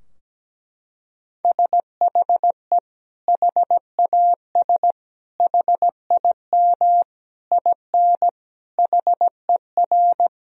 Morse code, heard as a keyed tone, about 17 wpm, 700 Hz. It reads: SHE HAS HIM IN HER
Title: Sentences from Top 1000 Words - Review Part 4 - 17wpm